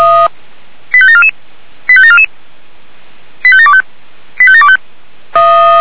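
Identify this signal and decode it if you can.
Five-tone selective call (ZVEI), heard as quick runs of five tones, tones of about 70 ms. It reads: 85744 85744 85725 85725
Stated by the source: Received by SDR, ZVEI five-tone selective call.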